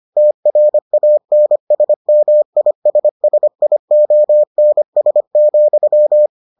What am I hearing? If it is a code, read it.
Morse code, heard as a keyed tone, about 25 wpm, 600 Hz. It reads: TRANSMISSIONS,